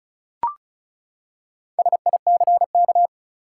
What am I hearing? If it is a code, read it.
Morse code, heard as a keyed tone, about 36 wpm, 700 Hz. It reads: SICK
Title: Four Letter Words - 35wpm